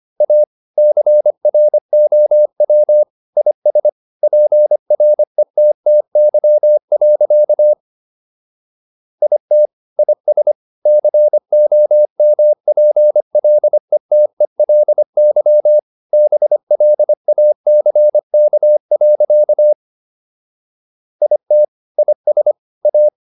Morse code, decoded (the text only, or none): A CROW IS PRETTY. IT IS COMPLETELY BLACK. IT IS A